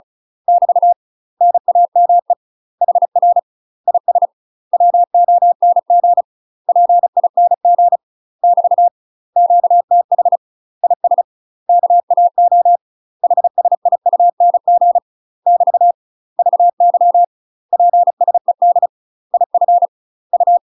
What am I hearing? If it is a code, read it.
Morse code, heard as a keyed tone, about 36 wpm, 700 Hz. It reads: <BT> NAME HR IS WONG PING <BT> QTH IS KAO HSIUNG <BT> VY PSED IF U
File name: UK MASTER QSOs TRAINING / 35WPM / UK MASTER QSOs 0007